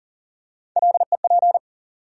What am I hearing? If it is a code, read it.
Morse code, heard as a keyed tone, about 40 wpm, 700 Hz. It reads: LEP
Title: USA Call Sign Postfixes - 3 Letters 40wpm